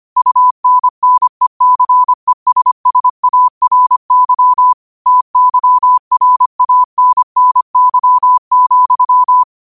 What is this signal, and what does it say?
Morse code, heard as a keyed tone, about 25 wpm, 1000 Hz. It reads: UNNECESSARY TYRANNY,
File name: qotdfiles / 25wpm